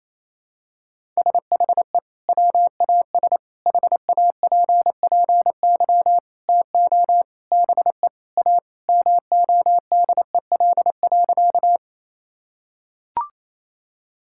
Morse code, heard as a keyed tone, about 28 wpm, 700 Hz. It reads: SHE WAS HAPPY TO BE A MODEL.
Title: Sentences from Top 1000 Words - Review Part 5 - 28wpm